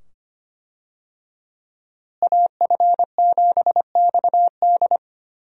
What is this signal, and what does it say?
Morse code, heard as a keyed tone, about 25 wpm, 700 Hz. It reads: AF7XD
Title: USA Call Signs - 5 Characters - Repeated 2x - 25wpm